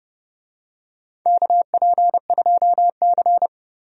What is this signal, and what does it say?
Morse code, heard as a keyed tone, about 30 wpm, 700 Hz. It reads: KP2C
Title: DXCC-100 Call Signs - 4 Characters - 30wpm